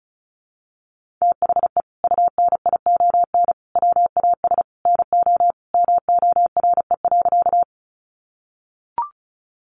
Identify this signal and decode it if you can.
Morse code, heard as a keyed tone, about 34 wpm, 700 Hz. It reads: THE UNION WAS NO MORE.
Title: Sentences from Top 1400 Words 35wpm